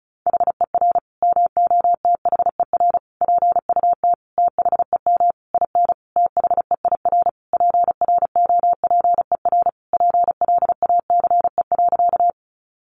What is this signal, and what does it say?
Morse code, heard as a keyed tone, about 34 wpm, 700 Hz. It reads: HER MOTHER PUT THEM IN THEIR PROPER PLACE.
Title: Sentences from Top 800 Words 35wpm